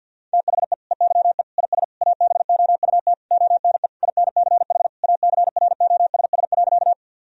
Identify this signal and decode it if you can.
Morse code, heard as a keyed tone, about 50 wpm, 700 Hz. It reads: THE EYE IS ABOUT ONE INCH ACROSS.
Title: Sentences from Top 1000 Words - Review Part 3 - 50wpm